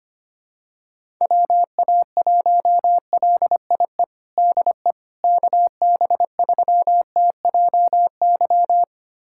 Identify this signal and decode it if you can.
Morse code, heard as a keyed tone, about 25 wpm, 700 Hz. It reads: WA1LIE DE KB3TJY
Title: USA Callsign DE Callsign - 6 Characters - 25wpm